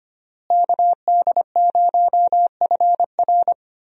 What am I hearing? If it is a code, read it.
Morse code, heard as a keyed tone, about 25 wpm, 700 Hz. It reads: KD0FR
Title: USA Call Signs - 5 Characters - Repeated 3x - 25wpm